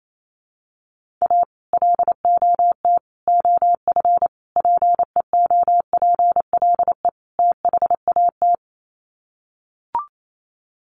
Morse code, heard as a keyed tone, about 28 wpm, 700 Hz. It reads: A LOT OF PEOPLE THAT
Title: Sets of 5 Words Encore2 28wpm